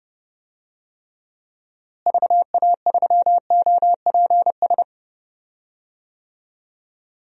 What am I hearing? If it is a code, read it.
Morse code, heard as a keyed tone, about 30 wpm, 700 Hz. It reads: VA3OPS
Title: DXCC-100 Call Signs - 6 Characters - Repeated 2x - 30wpm